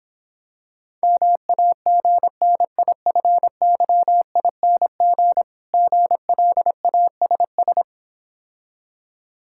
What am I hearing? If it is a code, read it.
Morse code, heard as a keyed tone, about 26 wpm, 690 Hz. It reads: MAGNIFYING GLASS